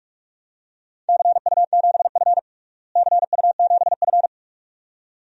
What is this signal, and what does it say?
Morse code, heard as a keyed tone, about 46 wpm, 700 Hz. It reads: CU7F CU7F
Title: DXCC-100 Call Signs - 4 Characters - Repeated 2x - 45wpm